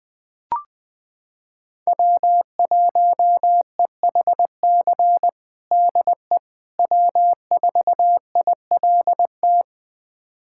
Morse code, heard as a keyed tone, about 20 wpm, 700 Hz. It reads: W1EHC DE W4ILT
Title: USA Callsign DE Callsign - 5 Characters - 20wpm